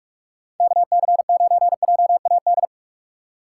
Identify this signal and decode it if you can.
Morse code, heard as a keyed tone, about 46 wpm, 700 Hz. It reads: KC9JAD
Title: USA Call Signs - 2x Character Spacing - 45wpm